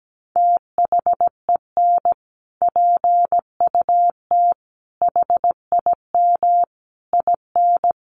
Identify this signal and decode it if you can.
Morse code, heard as a keyed tone, about 17 wpm, 700 Hz. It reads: THEN PUT HIM IN
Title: Sentences from Top 1500 Words 17wpm